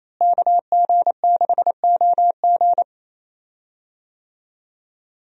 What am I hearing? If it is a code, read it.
Morse code, heard as a keyed tone, about 28 wpm, 700 Hz. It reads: KG6OG
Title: USA Call Signs - 5 Characters - Repeated 2x - 28wpm